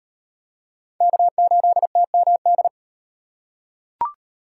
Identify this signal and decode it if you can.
Morse code, heard as a keyed tone, about 38 wpm, 700 Hz. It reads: K8TMD